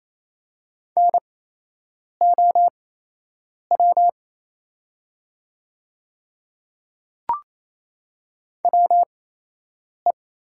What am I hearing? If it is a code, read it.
Morse code, heard as a keyed tone, about 28 wpm, spaced as for 8 wpm, 700 Hz. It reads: NOW WE
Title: 8x Character Spacing - Top 100 Words - 28wpm